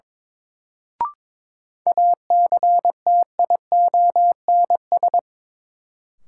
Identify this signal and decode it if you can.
Morse code, heard as a keyed tone, about 22 wpm, 700 Hz. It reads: ACTIONS
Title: Sentences from Top 1000 Words - Review Part 6 - New Word Drill - 22wpm